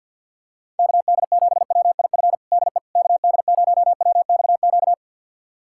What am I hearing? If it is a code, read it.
Morse code, heard as a keyed tone, about 51 wpm, 700 Hz. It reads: KD7WIF DE KD0WXQ
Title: USA Callsign DE Callsign - 6 Characters - 50wpm